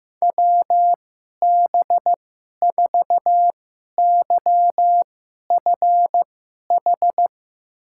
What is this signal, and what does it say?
Morse code, heard as a keyed tone, about 15 wpm, spaced as for 11 wpm, 700 Hz. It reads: WB4YFH
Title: USA Call Signs - 6 Characters - 2x Character Spacing - 15wpm